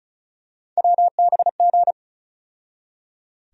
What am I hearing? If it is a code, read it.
Morse code, heard as a keyed tone, about 34 wpm, 700 Hz. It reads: WBG
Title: USA Call Sign Postfixes - 3 Letters 35wpm